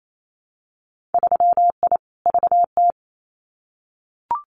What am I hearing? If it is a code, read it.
Morse code, heard as a keyed tone, about 28 wpm, 700 Hz. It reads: 3I VT
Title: Winter Field Day Exchanges Encore 28wpm